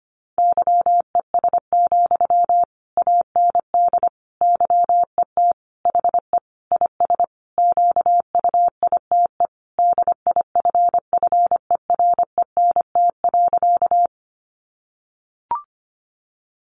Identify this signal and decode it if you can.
Morse code, heard as a keyed tone, about 25 wpm, 700 Hz. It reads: YES, AND YET HE IS QUITE DIFFERENT.
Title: Sentences from Top 700 Words 25wpm